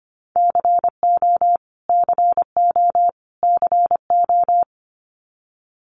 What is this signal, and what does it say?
Morse code, heard as a keyed tone, about 25 wpm, 700 Hz. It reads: CO CO CO
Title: US States Repeated 3x 25wpm